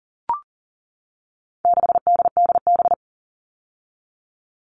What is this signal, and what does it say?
Morse code, heard as a keyed tone, about 40 wpm, 700 Hz. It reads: 6DDB
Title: All Cut Numbers - 4-Digit 40wpm